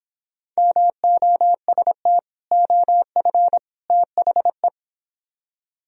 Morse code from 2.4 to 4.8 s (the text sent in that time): OF THE